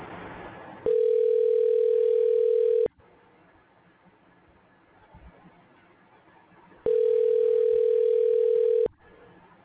Ringback tone (North America), 440 + 480 Hz.